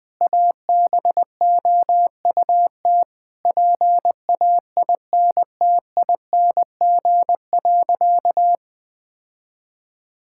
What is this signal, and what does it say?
Morse code, heard as a keyed tone, about 21 wpm, 700 Hz. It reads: ABOUT PAINTING.
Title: Sentences from Top 1000 Words - Review Part 6 - 20wpm